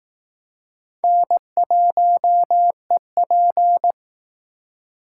Morse code, heard as a keyed tone, about 20 wpm, 700 Hz. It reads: N1EP